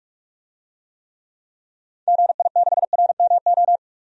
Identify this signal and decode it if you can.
Morse code, heard as a keyed tone, about 42 wpm, 700 Hz. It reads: GI6RMO
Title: DXCC-100 Call Signs - 6 Characters - 45wpm